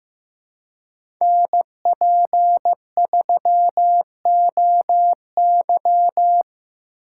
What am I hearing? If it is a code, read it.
Morse code, heard as a keyed tone, about 15 wpm, 700 Hz. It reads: NP3OY
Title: USA Call Signs - 5 Characters - 4x Character Spacing - 15wpm